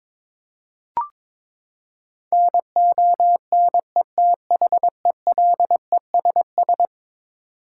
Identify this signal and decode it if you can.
Morse code, heard as a keyed tone, about 22 wpm, 700 Hz. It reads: NONETHELESS
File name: practice / 22wpm / Eleven Letter Words - 22wpm